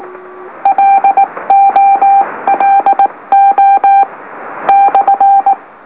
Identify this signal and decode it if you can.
Morse code, heard as a keyed tone, about 19 wpm, spaced as for 17 wpm, 790 Hz. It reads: LOLO /
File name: M04- -2345utc-LOLO